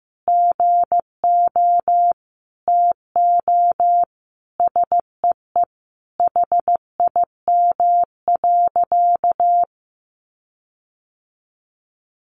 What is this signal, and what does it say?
Morse code, heard as a keyed tone, about 15 wpm, 700 Hz. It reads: GO TO SEE HIM.
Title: Sentences from Top 1500 Words 15wpm